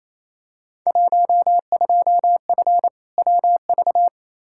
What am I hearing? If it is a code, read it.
Morse code, heard as a keyed tone, about 28 wpm, 700 Hz. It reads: 12F WV
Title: Field Day Exchanges Encore - 28wpm